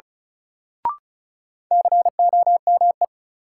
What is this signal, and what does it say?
Morse code, heard as a keyed tone, about 36 wpm, 700 Hz. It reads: COME
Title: Sets of 2 Words - New Word Drill - 35wpm